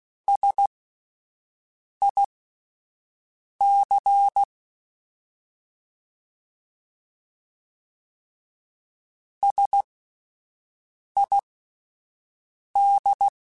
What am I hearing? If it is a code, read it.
Morse code, heard as a keyed tone, about 16 wpm, spaced as for 5 wpm, 800 Hz. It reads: SIC SID